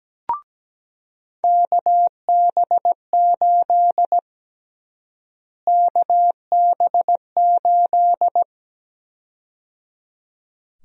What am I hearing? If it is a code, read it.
Morse code, heard as a keyed tone, about 17 wpm, 700 Hz. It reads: KB8 KB8
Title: USA Call Sign Prefixes - Repeated 2x - 17wpm